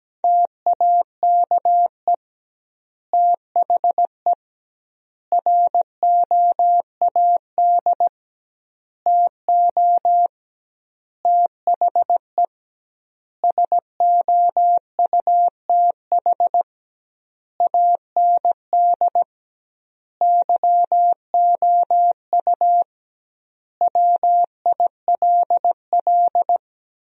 Morse code, heard as a keyed tone, about 17 wpm, 700 Hz. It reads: TAKE THE ROAD TO THE SOUTH AND YOU WILL
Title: 2x Word-Spacing - Sentences from Top 400 Words - 17wpm